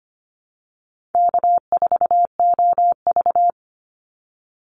Morse code, heard as a keyed tone, about 25 wpm, 700 Hz. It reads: K4OV
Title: Call Signs Repeated 2x - 25wpm